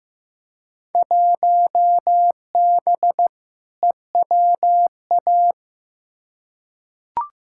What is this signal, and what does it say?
Morse code, heard as a keyed tone, about 15 wpm, 700 Hz. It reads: 1B EWA
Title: Field Day Exchanges Encore - 15wpm